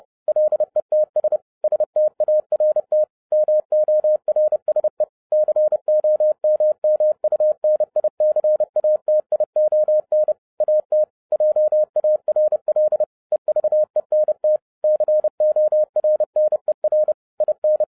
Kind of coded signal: Morse code